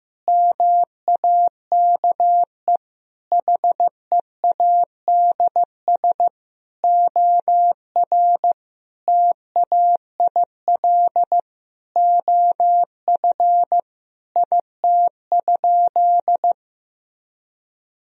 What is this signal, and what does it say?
Morse code, heard as a keyed tone, about 15 wpm, 700 Hz. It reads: MAKE HEADS OR TAIL OF IT?
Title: Sentences from Top 1000 Words - Review Part 3 - 15wpm